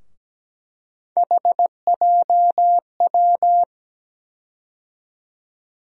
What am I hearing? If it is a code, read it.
Morse code, heard as a keyed tone, about 17 wpm, 700 Hz. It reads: HJW
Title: USA Call Sign Postfixes - 3 Letters 17wpm